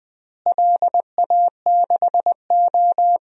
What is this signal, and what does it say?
Morse code, heard as a keyed tone, about 20 wpm, 700 Hz. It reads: LA6O
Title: DXCC-100 Call Signs - 4 Characters - Repeated 2x - 20wpm